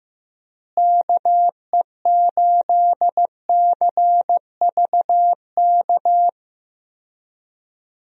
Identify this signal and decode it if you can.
Morse code, heard as a keyed tone, about 15 wpm, 700 Hz. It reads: KE8CVK
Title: USA Call Signs - 6 Characters - Repeated 2x - 15wpm